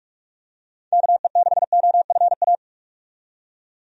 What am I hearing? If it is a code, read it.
Morse code, heard as a keyed tone, about 46 wpm, 700 Hz. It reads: KE6OFA
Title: USA Call Signs - 6 Characters - 45wpm